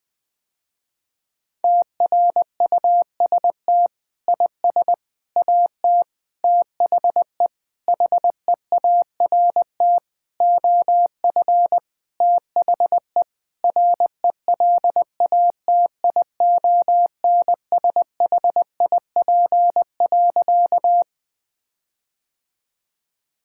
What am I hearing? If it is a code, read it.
Morse code, heard as a keyed tone, about 19 wpm, 700 Hz. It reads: TRUST IS AT THE HEART OF THE RELATIONSHIP.
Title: Sentences with 6-Word Phrases - Spoken Beginning - 20wpm